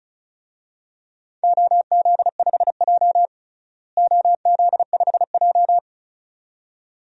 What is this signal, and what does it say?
Morse code, heard as a keyed tone, about 36 wpm, 700 Hz. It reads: OZ5J OZ5J
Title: DXCC-100 Call Signs - 4 Characters - Repeated 2x - 35wpm